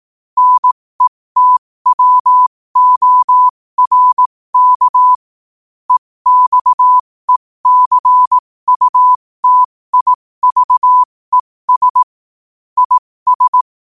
Morse code, heard as a keyed tone, about 18 wpm, spaced as for 15 wpm, 1000 Hz. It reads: NETWORK EXECUTIVES IS
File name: qotdfiles / 15wpm